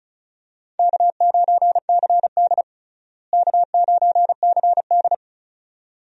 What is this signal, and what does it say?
Morse code, heard as a keyed tone, about 36 wpm, 700 Hz. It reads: K9CD K9CD
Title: USA Call Signs - 4 Characters - Repeated 2x - 35wpm